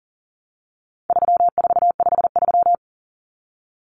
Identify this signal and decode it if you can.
Morse code, heard as a keyed tone, about 40 wpm, 700 Hz. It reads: 3453
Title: Numbers - 4-Digit 40wpm